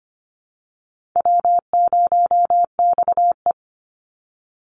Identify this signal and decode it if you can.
Morse code, heard as a keyed tone, about 25 wpm, 700 Hz. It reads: W0XE
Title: Easy Call Signs - 25wpm